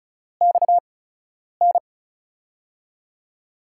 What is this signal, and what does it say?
Morse code, heard as a keyed tone, about 36 wpm, spaced as for 10 wpm, 700 Hz. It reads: XN
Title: USA Call Signs - 5 Characters - 8x Character Spacing - 35wpm